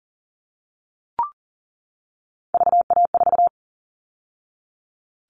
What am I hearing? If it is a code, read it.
Morse code, heard as a keyed tone, about 40 wpm, 700 Hz. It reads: VA4
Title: All Cut Numbers - 3-Digit 40wpm